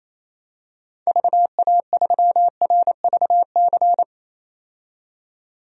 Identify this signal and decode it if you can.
Morse code, heard as a keyed tone, about 28 wpm, 700 Hz. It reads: VA3RVC
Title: DXCC-100 Call Signs - 6 Characters - 28wpm